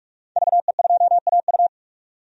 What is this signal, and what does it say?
Morse code, heard as a keyed tone, about 45 wpm, 700 Hz. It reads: VE2AU
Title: DXCC-100 Call Signs - 5 Characters - 45wpm